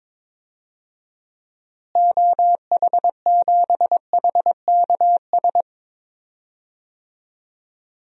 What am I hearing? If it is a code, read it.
Morse code, heard as a keyed tone, about 22 wpm, 700 Hz. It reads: OH7HKS
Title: DXCC-100 Call Signs - 6 Characters - Repeated 3x - 22wpm